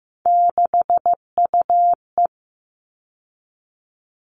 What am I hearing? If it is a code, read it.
Morse code, heard as a keyed tone, about 15 wpm, 700 Hz. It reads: 6UE
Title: All Cut Numbers - 3-Digit 15wpm